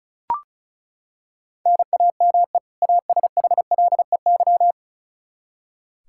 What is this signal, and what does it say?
Morse code, heard as a keyed tone, about 36 wpm, 700 Hz. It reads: NAME ASHLEY
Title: Practice QSO - Exchange 1 - Beginning - 35wpm